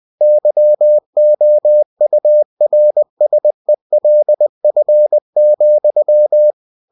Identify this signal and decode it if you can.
Morse code, heard as a keyed tone, about 20 wpm, 600 Hz. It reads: YOURSELF,
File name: cw-books / winnie-the-pooh / winnie-20wpm-page-0055